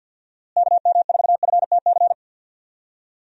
Morse code, heard as a keyed tone, about 52 wpm, 700 Hz. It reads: KM4FTC
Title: USA Call Signs - 6 Characters - Repeated 3x - 50wpm